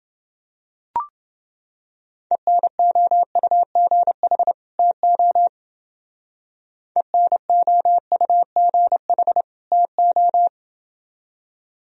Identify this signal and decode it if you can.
Morse code, heard as a keyed tone, about 29 wpm, 700 Hz. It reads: ENOUGH TO ENOUGH TO